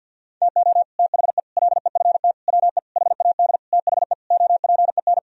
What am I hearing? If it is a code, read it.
Morse code, heard as a keyed tone, about 50 wpm, 700 Hz. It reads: TO THE LEFT WE HAD THE OPEN
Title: Sentences from Top 1000 Words - Review Part 1 - 50wpm